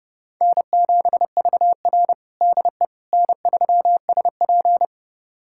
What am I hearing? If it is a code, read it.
Morse code, heard as a keyed tone, about 32 wpm, 700 Hz. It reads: N7VR DE N3SP
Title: USA Callsign DE Callsign - 4 Characters - 30wpm